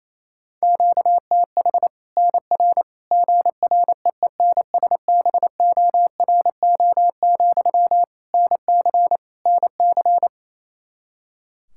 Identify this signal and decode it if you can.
Morse code, heard as a keyed tone, about 28 wpm, 700 Hz. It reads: QTH NR GREENSBORO, NC NC
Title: Practice QSO - Exchange 1 - Beginning - 28wpm